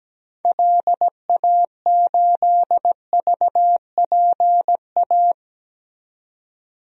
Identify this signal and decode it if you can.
Morse code, heard as a keyed tone, about 17 wpm, 700 Hz. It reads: LA8VPA
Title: DXCC-100 Call Signs - 6 Characters - 17wpm